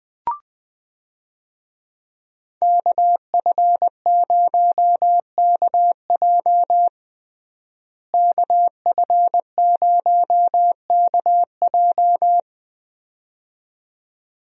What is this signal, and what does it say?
Morse code, heard as a keyed tone, about 20 wpm, 700 Hz. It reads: KF0KJ KF0KJ
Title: USA Call Signs - 5 Characters - Repeated 2x - 20wpm